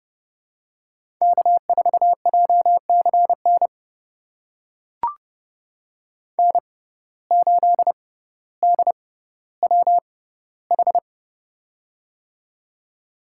Morse code, heard as a keyed tone, about 32 wpm, spaced as for 16 wpm, 700 Hz. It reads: K4JCN N8DWH
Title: USA Call Signs - 5 Characters - 6x Character Spacing - 30wpm